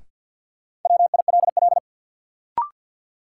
Morse code, heard as a keyed tone, about 48 wpm, 700 Hz. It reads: WILL
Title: Sentences from Top 100 Words - New Word Drill - 50wpm